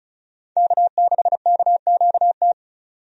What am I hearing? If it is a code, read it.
Morse code, heard as a keyed tone, about 34 wpm, 700 Hz. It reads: K6KQT
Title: USA Call Signs - 5 Characters - 35wpm